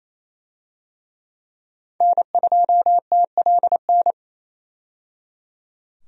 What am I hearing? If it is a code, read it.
Morse code, heard as a keyed tone, about 28 wpm, 700 Hz. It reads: N2TLN
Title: USA Call Signs - 5 Characters - 28wpm